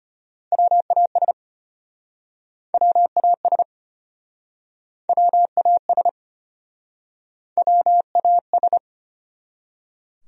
Morse code, heard as a keyed tone, about 34 wpm, 700 Hz. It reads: WAS WAS WAS WAS